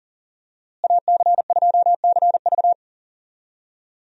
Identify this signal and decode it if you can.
Morse code, heard as a keyed tone, about 40 wpm, 700 Hz. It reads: AC2CV
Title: USA Call Signs - 5 Characters - Repeated 3x - 40wpm